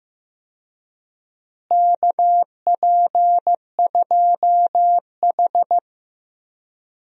Morse code, heard as a keyed tone, about 14 wpm, 700 Hz. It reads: KP2H
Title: DXCC-100 Call Signs - 4 Characters - Repeated 2x - 15wpm